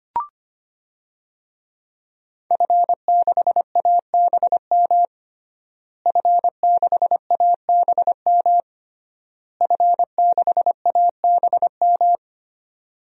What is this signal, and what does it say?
Morse code, heard as a keyed tone, about 25 wpm, 700 Hz. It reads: F6ABM F6ABM F6ABM
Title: DXCC-100 Call Signs - 5 Characters - Repeated 3x - 25wpm